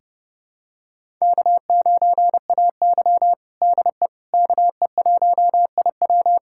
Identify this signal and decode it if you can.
Morse code, heard as a keyed tone, about 30 wpm, 700 Hz. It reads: K9AY DE KE1IW
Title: USA Callsign DE Callsign 30wpm